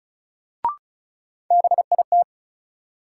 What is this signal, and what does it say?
Morse code, heard as a keyed tone, about 34 wpm, 700 Hz. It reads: BIT
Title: Sentences from Top 900 Words - New Word Drill - 35wpm